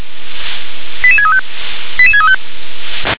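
Five-tone selective call (ZVEI), heard as quick runs of five tones, tones of about 70 ms. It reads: 88635 88635
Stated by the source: ZVEI five-tone selective call, received by SDR.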